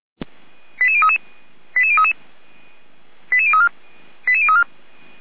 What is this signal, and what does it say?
Five-tone selective call (ZVEI), heard as quick runs of five tones, tones of about 70 ms. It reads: 88033 88033 88034 88034
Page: /455/SDR/ZVEI/88391/2023.11.20/